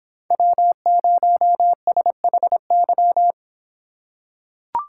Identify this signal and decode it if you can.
Morse code, heard as a keyed tone, about 26 wpm, 700 Hz. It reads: W0SHY